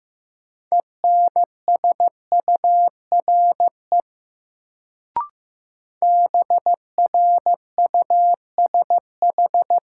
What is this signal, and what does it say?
Morse code, heard as a keyed tone, about 16 wpm, 700 Hz. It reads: ENSURE BRUSH
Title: Sentences from Top 1100 Words - New Word Drill - 15wpm